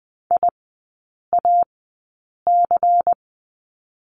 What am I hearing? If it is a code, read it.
Morse code, heard as a keyed tone, about 20 wpm, 700 Hz. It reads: I A C
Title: 133 Lesson 22e - Question Mark - Three Characters 20wpm